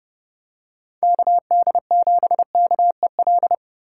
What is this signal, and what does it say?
Morse code, heard as a keyed tone, about 30 wpm, 700 Hz. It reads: KD7KEL
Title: USA Call Signs - 6 Characters - 2x Character Spacing - 30wpm